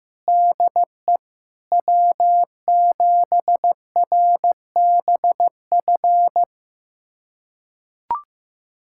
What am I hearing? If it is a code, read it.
Morse code, heard as a keyed tone, about 16 wpm, 700 Hz. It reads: DE W7RBF
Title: USA Callsign DE Callsign - 5 Characters - 15wpm